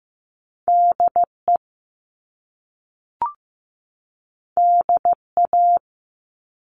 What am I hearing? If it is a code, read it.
Morse code, heard as a keyed tone, about 15 wpm, 700 Hz. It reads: DE DA
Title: Top Two Letter Combinations in Words 15wpm